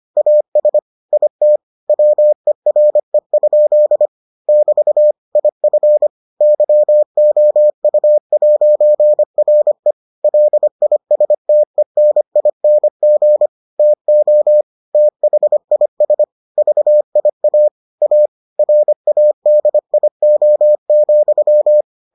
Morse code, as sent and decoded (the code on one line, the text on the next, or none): .- ... / .. - / .-- . .-. ..--.. / -...- / .. ..-. / -.-- --- ..- .----. .-. / .-.. .. ... - . -. .. -. --. / - --- / - .... .. ... / ...- .. .- / .- / .-. .- -.. .. --- --..--
AS IT WERE? <BT> IF YOU'RE LISTENING TO THIS VIA A RADIO,